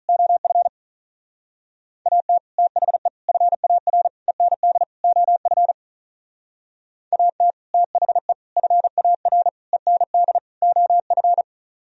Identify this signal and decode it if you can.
Morse code, heard as a keyed tone, about 40 wpm, 700 Hz. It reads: OF AT THE FAR END OF AT THE FAR END OF
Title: Speed-Racing - Sets of 5 Words 35wpm